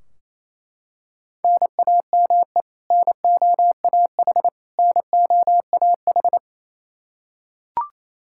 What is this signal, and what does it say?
Morse code, heard as a keyed tone, about 27 wpm, 700 Hz. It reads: NAME NOAH NOAH
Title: Practice QSO - Exchange 1 - Beginning - 28wpm